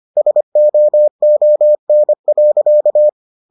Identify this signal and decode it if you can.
Morse code, heard as a keyed tone, about 25 wpm, 600 Hz. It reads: SOON.